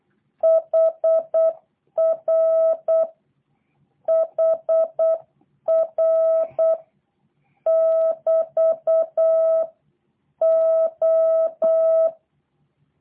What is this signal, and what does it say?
Morse code, heard as a keyed tone, about 8 wpm, 650 Hz. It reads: HR HR <BT> O
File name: jh8cbh / 2023cw / review1-4